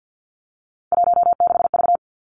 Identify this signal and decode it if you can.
Morse code, heard as a keyed tone, about 50 wpm, 700 Hz. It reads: 16V
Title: Three Challenging Letter-Number Combinations 50wpm